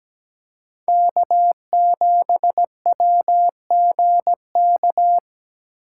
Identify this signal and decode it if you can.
Morse code, heard as a keyed tone, about 17 wpm, 700 Hz. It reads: K7WGK
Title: USA Call Signs - 5 Characters - 6x Character Spacing - 17wpm